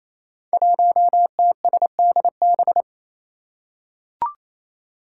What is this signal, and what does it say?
Morse code, heard as a keyed tone, about 27 wpm, 700 Hz. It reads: 1TSDB
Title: Five Challenging Letter-Number Combinations 28wpm